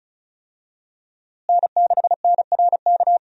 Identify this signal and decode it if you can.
Morse code, heard as a keyed tone, about 34 wpm, 700 Hz. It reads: N6NRK
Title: USA Call Signs - 5 Characters - Repeated 2x - 35wpm